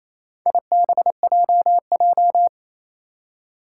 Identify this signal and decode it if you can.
Morse code, heard as a keyed tone, about 28 wpm, 700 Hz. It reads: IBJJ
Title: Four Challenging Letter-Number Combinations 28wpm